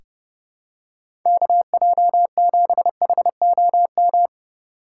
Morse code, heard as a keyed tone, about 30 wpm, 700 Hz. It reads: KJ7HOM
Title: USA Call Signs - 6 Characters - 2x Character Spacing - 30wpm